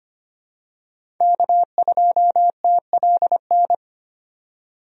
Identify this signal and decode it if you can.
Morse code, heard as a keyed tone, about 25 wpm, 700 Hz. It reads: K2TLN
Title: USA Call Signs - 5 Characters - 25wpm